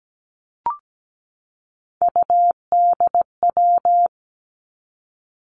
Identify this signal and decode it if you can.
Morse code, heard as a keyed tone, about 17 wpm, 700 Hz. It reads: UDW
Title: Three Challenging Letter-Number Combinations 17wpm